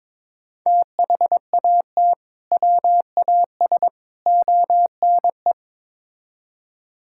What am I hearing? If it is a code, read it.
Morse code, heard as a keyed tone, about 22 wpm, 700 Hz. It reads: THAT WAS ONE
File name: practice / 22wpm / Build A Phrase - 5-Word Phrases - 22wpm